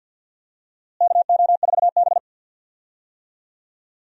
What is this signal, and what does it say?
Morse code, heard as a keyed tone, about 51 wpm, 700 Hz. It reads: KO4B
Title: USA Call Signs - 4 Characters - Repeated 3x - 50wpm